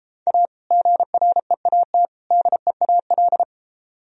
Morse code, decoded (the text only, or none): A GREAT DEAL